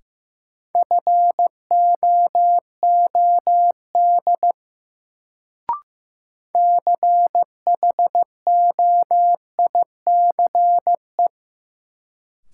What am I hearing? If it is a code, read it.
Morse code, heard as a keyed tone, about 15 wpm, 700 Hz. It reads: FOOD CHOICE
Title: Sets of 5 Words - New Word Drill - 15wpm